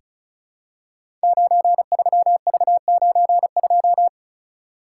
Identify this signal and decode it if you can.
Morse code, heard as a keyed tone, about 36 wpm, 700 Hz. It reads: 93V92
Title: Code Groups - 5 Letter:Numbers - 35wpm